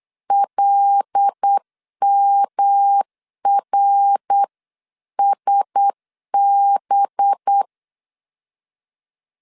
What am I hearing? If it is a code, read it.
Morse code, heard as a keyed tone, about 8 wpm, 800 Hz. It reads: LMRSB
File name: 8wpm